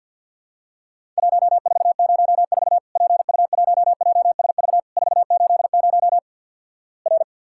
Morse code, heard as a keyed tone, about 50 wpm, 700 Hz. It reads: 1404 PU1JSV 480 R
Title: Worked All Europe DX Contest - QTC Exchange - 50wpm